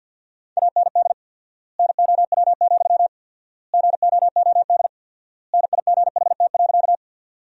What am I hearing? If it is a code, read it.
Morse code, heard as a keyed tone, about 48 wpm, 700 Hz. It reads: AND NOW, GOOD NIGHT.
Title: 4x Word-Spacing - Sentences from Top 300 Words - 50wpm